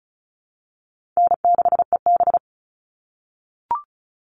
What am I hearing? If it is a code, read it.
Morse code, heard as a keyed tone, about 34 wpm, 700 Hz. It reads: N6EB
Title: All Cut Numbers - 4-Digit 35wpm